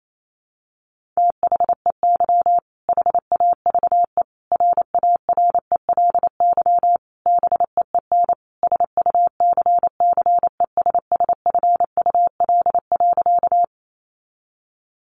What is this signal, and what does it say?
Morse code, heard as a keyed tone, about 28 wpm, 700 Hz. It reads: THEY HAVE RARELY BEEN SUCCESSFUL.
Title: Sentences from Top 1400 Words 28wpm